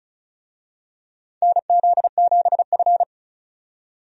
Morse code, heard as a keyed tone, about 36 wpm, 700 Hz. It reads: NZ7F
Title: USA Call Signs - 4 Characters - 35wpm